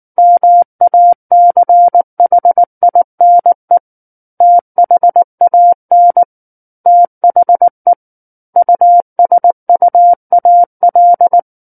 Morse code, heard as a keyed tone, about 19 wpm, 700 Hz. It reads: MACHINE THAN THE USUAL
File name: Book One / 10-Chapter 10 - 19 WPM